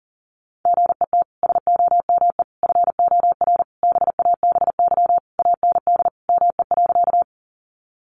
Morse code, heard as a keyed tone, about 40 wpm, 700 Hz. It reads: GET SOME FOR BABY AND ME.
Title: Sentences from Top 1300 Words 40wpm